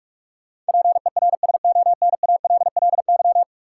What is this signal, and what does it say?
Morse code, heard as a keyed tone, about 46 wpm, 700 Hz. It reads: PERSONALLY
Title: Sentences from Top 1400 Words - New Word Drill - 45wpm